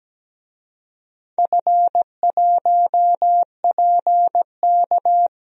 Morse code, heard as a keyed tone, about 17 wpm, 700 Hz. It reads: F1PK
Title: DXCC-100 Call Signs - 4 Characters - Repeated 3x - 17wpm